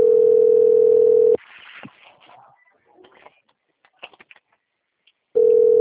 Ringback tone (North America), 440 + 480 Hz.